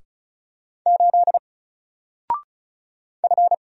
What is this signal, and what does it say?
Morse code, heard as a keyed tone, about 36 wpm, 700 Hz. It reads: 8 F
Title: Single Letter-Number 35wpm